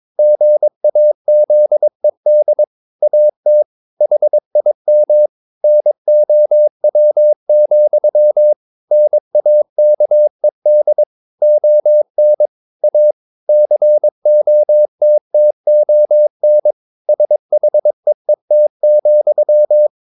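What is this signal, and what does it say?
Morse code, heard as a keyed tone, about 22 wpm, 600 Hz. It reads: GAZED AT HIM NOW, NAKED ON A COTTON SHEET,
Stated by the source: Morse code practice at 22 wpm, sending text from an English novel.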